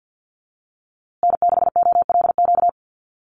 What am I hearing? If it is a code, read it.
Morse code, heard as a keyed tone, about 50 wpm, 700 Hz. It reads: N6OLQ
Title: USA Call Signs Repeated 3x 50wpm